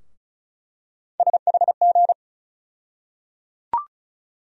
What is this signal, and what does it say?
Morse code, heard as a keyed tone, about 34 wpm, 700 Hz. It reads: SHG
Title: USA Call Sign Postfixes - 3 Letters 35wpm